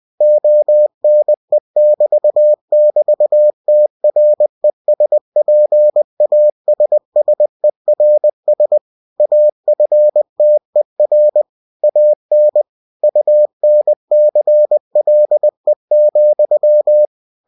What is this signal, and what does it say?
Morse code, heard as a keyed tone, about 20 wpm, 600 Hz. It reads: ONE<BT><BT>TRESPASSERS AFTER AN UNCLE,
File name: cw-books / winnie-the-pooh / winnie-20wpm-page-0020